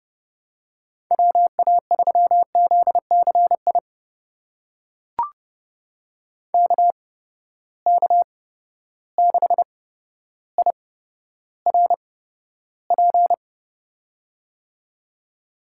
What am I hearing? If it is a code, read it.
Morse code, heard as a keyed tone, about 30 wpm, spaced as for 13 wpm, 700 Hz. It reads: WA3ZCI KK6IRP
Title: USA Call Signs - 8x Character Spacing - 30wpm